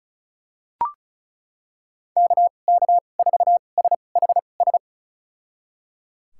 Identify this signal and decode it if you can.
Morse code, heard as a keyed tone, about 34 wpm, spaced as for 25 wpm, 700 Hz. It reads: KK4SHS
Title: USA Call Signs - 6 Characters - 2x Character Spacing - 35wpm